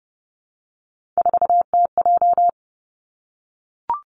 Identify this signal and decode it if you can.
Morse code, heard as a keyed tone, about 32 wpm, 700 Hz. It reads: 4TJ
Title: Three Challenging Letter-Number Combinations 30wpm